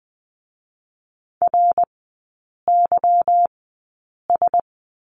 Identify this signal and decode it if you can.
Morse code, heard as a keyed tone, about 20 wpm, spaced as for 8 wpm, 700 Hz. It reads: RYS
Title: 178 Lesson 29e - Slash - Three Characters 20wpm